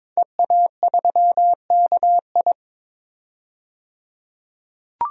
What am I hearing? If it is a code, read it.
Morse code, heard as a keyed tone, about 22 wpm, 700 Hz. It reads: EA3KI
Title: DXCC-100 Call Signs - 5 Characters - 22wpm